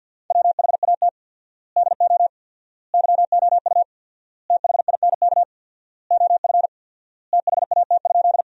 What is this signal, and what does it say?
Morse code, heard as a keyed tone, about 50 wpm, 700 Hz. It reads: WHAT DO YOU THINK OF THAT?
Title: 4x Word-Spacing - Sentences from Top 200 Words - 50wpm